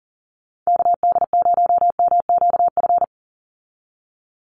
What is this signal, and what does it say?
Morse code, heard as a keyed tone, about 40 wpm, 700 Hz. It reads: KD0MQF
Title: USA Call Signs 40wpm